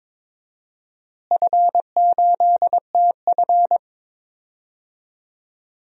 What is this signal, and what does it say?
Morse code, heard as a keyed tone, about 22 wpm, 700 Hz. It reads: F8TF
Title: DXCC-100 Call Signs - 4 Characters - Repeated 2x - 22wpm